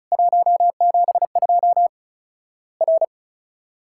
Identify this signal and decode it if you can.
Morse code, heard as a keyed tone, about 34 wpm, 700 Hz. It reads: 172 R
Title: Worked All Europe DX Contest - QTC Exchange - 35wpm